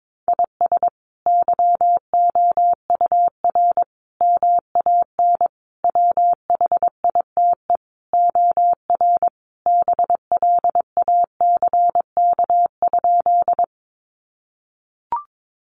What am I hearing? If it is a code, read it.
Morse code, heard as a keyed tone, about 22 wpm, 700 Hz. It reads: IS YOUR MAN WHITE OR BLACK?
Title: Sentences from Top 400 Words 22wpm